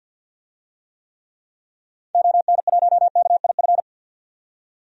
Morse code, heard as a keyed tone, about 50 wpm, 700 Hz. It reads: ON1KIF
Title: DXCC-100 Call Signs - 6 Characters - 50wpm